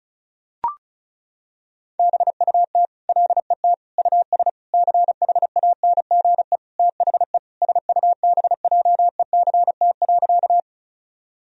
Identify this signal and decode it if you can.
Morse code, heard as a keyed tone, about 34 wpm, 700 Hz. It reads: BUT LET US CHANGE THE SUBJECT.
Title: Sentences from Top 1000 Words - Review Part 1 - 35wpm